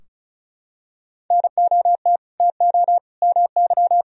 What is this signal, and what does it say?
Morse code, heard as a keyed tone, about 34 wpm, 700 Hz. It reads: NOT TO MY
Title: Sentences from Top 1000 Words - Review Part 4 - 35wpm